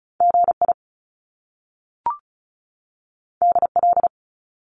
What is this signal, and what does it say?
Morse code, heard as a keyed tone, about 36 wpm, 700 Hz. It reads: GI DL
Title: Top Two Letter Combinations in Words 35wpm